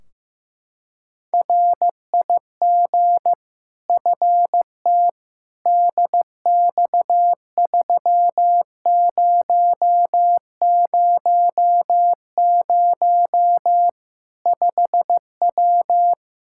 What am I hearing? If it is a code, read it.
Morse code, heard as a keyed tone, about 15 wpm, 700 Hz. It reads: RIG FT DX3000 5W
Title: K1USN SST QSO Practice 15wpm